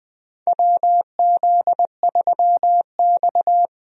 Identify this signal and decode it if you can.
Morse code, heard as a keyed tone, about 20 wpm, 700 Hz. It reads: WZ3X
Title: USA Call Signs - 4 Characters - Repeated 2x - 20wpm